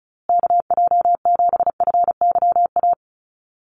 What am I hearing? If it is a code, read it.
Morse code, heard as a keyed tone, about 34 wpm, 700 Hz. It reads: KJ7FYA
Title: USA Call Signs Repeated 2x 35wpm